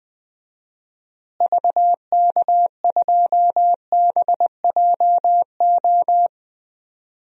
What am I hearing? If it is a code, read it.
Morse code, heard as a keyed tone, about 20 wpm, 700 Hz. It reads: VK2BJO
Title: DXCC-100 Call Signs - 6 Characters - Repeated 2x - 20wpm